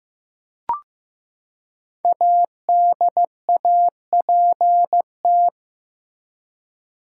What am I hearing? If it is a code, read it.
Morse code, heard as a keyed tone, about 15 wpm, 700 Hz. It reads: ADAPT
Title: Five Letter Words - 15wpm